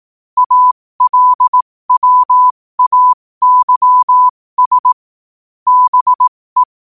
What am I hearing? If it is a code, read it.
Morse code, heard as a keyed tone, about 18 wpm, spaced as for 15 wpm, 1000 Hz. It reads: ALWAYS BE
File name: qotdfiles / 15wpm